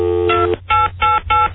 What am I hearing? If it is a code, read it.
DTMF, heard as telephone touch tones, tones of about 180 ms. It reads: 2888